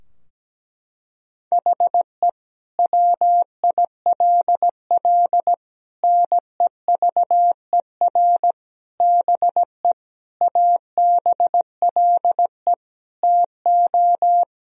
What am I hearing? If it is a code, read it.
Morse code, heard as a keyed tone, about 17 wpm, 700 Hz. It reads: HE WILL NEVER BE ABLE TO
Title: Sentences from Top 1000 Words - Review Part 2 - 17wpm